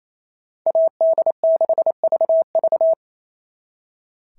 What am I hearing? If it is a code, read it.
Morse code, heard as a keyed tone, about 28 wpm, 650 Hz. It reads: AD6VV
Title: SOTA QSO Practice 28wpm